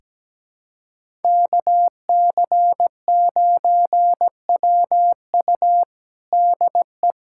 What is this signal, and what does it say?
Morse code, heard as a keyed tone, about 17 wpm, 700 Hz. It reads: KC9WU DE
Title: USA Callsign DE Callsign - 5 Characters - 17wpm